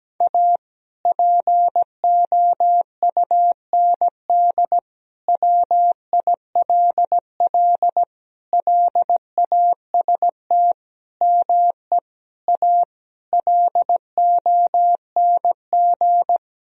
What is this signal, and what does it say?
Morse code, heard as a keyed tone, about 17 wpm, 700 Hz. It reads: A POUND WILL LAST ME A LONG
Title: Sentences from Top 1000 Words - Review Part 2 - 17wpm